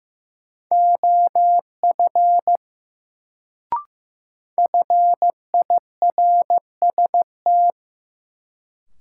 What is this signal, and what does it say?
Morse code, heard as a keyed tone, about 15 wpm, 700 Hz. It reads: OF FIRST
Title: Sentences from Top 100 Words - New Word Drill - 15wpm